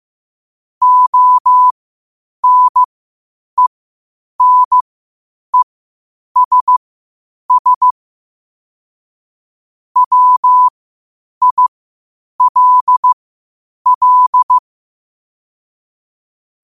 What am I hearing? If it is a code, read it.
Morse code, heard as a keyed tone, about 15 wpm, spaced as for 8 wpm, 1000 Hz. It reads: ONENESS WILL
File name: qotdfiles / 8wpm